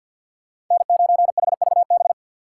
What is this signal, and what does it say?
Morse code, heard as a keyed tone, about 51 wpm, 700 Hz. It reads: N9HVB
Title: USA Call Signs - 5 Characters - 4x Character Spacing - 50wpm